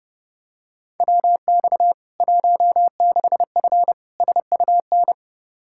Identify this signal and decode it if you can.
Morse code, heard as a keyed tone, about 30 wpm, 700 Hz. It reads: WX 16F SUN
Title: K1USN SST QSO Practice 30wpm